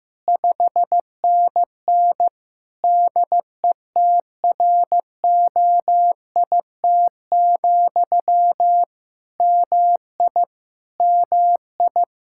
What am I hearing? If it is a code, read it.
Morse code, heard as a keyed tone, about 15 wpm, 700 Hz. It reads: 5NN DETROIT, MI MI
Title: Practice QSO - Exchange 1 - Beginning - 15wpm